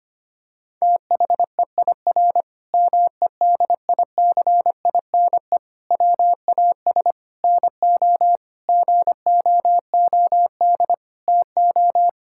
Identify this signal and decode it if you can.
Morse code, heard as a keyed tone, about 25 wpm, 700 Hz. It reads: THEIR MEDICINE WAS NO GOOD TO
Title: Sentences from Top 1000 Words - Review Part 6 - 25wpm